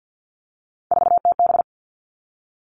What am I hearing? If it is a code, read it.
Morse code, heard as a keyed tone, about 50 wpm, 700 Hz. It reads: 4TB
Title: All Cut Numbers - 3-Digit 50wpm